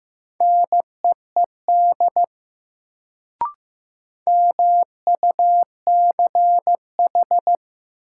Morse code, heard as a keyed tone, about 15 wpm, 700 Hz. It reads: NEED MUCH
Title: Sentences from Top 200 Words - New Word Drill - 15wpm